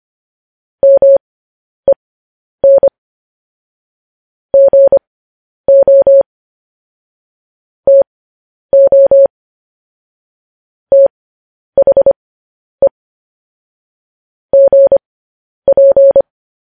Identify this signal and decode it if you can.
Morse code, heard as a keyed tone, about 25 wpm, spaced as for 10 wpm, 550 Hz. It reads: MEN GO TO THE GP
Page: News Headlines in Morse Code at 10 WPM